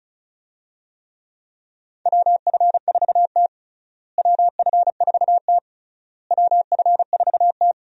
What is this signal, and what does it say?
Morse code, heard as a keyed tone, about 34 wpm, 700 Hz. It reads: WF4T WF4T WF4T
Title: USA Call Signs - 4 Characters - Repeated 3x - 35wpm